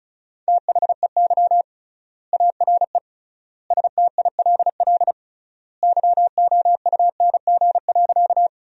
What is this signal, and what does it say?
Morse code, heard as a keyed tone, about 34 wpm, 700 Hz. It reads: THEY ARE STILL YOUNG.
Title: 3x Word-Spacing - Sentences from Top 400 Words - 35wpm